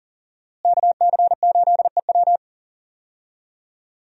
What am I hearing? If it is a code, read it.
Morse code, heard as a keyed tone, about 40 wpm, 700 Hz. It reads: KC8EW
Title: USA Call Signs - 5 Characters - Repeated 3x - 40wpm